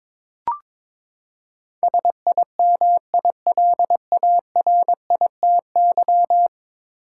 Morse code, heard as a keyed tone, about 22 wpm, 700 Hz. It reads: SIMILARITY
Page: morse-code-ninja-practice-22wpm